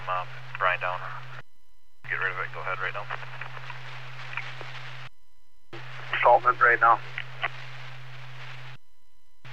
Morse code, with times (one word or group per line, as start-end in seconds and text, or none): none